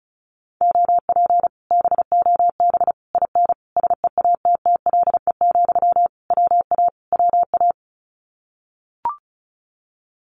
Morse code, heard as a keyed tone, about 36 wpm, 700 Hz. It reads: OP BOB IN SEATTLE, WA WA
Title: Common QSO Phrases Encore 35wpm